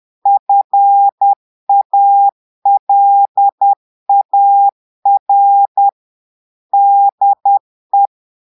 Morse code, heard as a keyed tone, about 10 wpm, 800 Hz. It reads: FALAR DE
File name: portugais texte en clair 2 vitesse 10 mots minutes